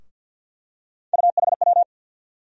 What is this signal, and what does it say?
Morse code, heard as a keyed tone, about 53 wpm, 700 Hz. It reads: UHW
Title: USA Call Sign Postfixes - 3 Letters 50wpm